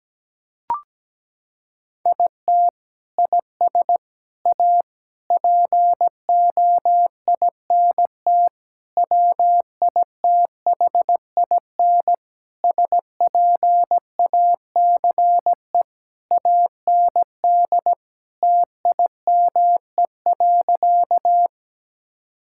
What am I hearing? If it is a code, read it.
Morse code, heard as a keyed tone, about 17 wpm, 700 Hz. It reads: IT IS A POINT WITHIN SPACE AND TIME.
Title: Sentences from Top 1000 Words - Review Part 5 - 17wpm